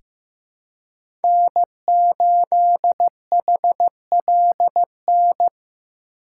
Morse code, heard as a keyed tone, about 15 wpm, 700 Hz. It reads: N8HLN
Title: USA Call Signs - 4x Character Spacing - 15wpm